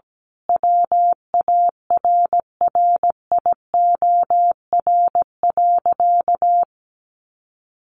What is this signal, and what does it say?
Morse code, heard as a keyed tone, about 17 wpm, 700 Hz. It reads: WARRIOR.
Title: A Sound Of Thunder 17wpm